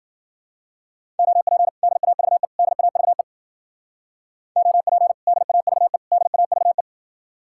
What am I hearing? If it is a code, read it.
Morse code, heard as a keyed tone, about 52 wpm, 700 Hz. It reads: OP DAVE DAVE OP DAVE DAVE